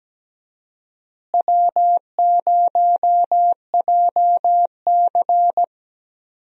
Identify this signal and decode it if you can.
Morse code, heard as a keyed tone, about 17 wpm, 700 Hz. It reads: W0JC
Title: USA Call Signs - 4 Characters - Repeated 3x - 17wpm